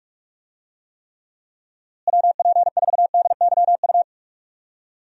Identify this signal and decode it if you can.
Morse code, heard as a keyed tone, about 41 wpm, 700 Hz. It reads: WP4DYU
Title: USA Call Signs - 6 Characters - Repeated 2x - 45wpm